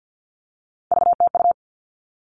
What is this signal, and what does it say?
Morse code, heard as a keyed tone, about 50 wpm, 700 Hz. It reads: VTU
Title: All Cut Numbers - 3-Digit 50wpm